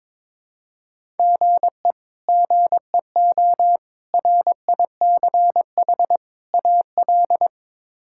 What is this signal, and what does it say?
Morse code, heard as a keyed tone, about 22 wpm, 700 Hz. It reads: GE GEO RICH AL
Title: K1USN SST QSO Practice - 22wpm